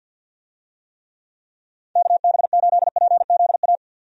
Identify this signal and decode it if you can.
Morse code, heard as a keyed tone, about 52 wpm, 700 Hz. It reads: KB8PZA
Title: USA Call Signs - 6 Characters - 50wpm